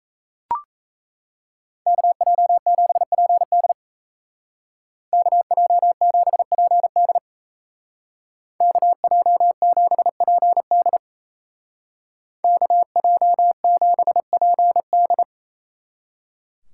Morse code, heard as a keyed tone, about 32 wpm, 700 Hz. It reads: KJ7PD KJ7PD KJ7PD KJ7PD
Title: Speed Racing - USA 2x2 Call Signs 28wpm